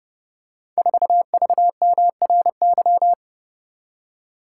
Morse code, heard as a keyed tone, about 30 wpm, 700 Hz. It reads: 4VMRY
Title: Code Groups - 5 Letter:Numbers - 30wpm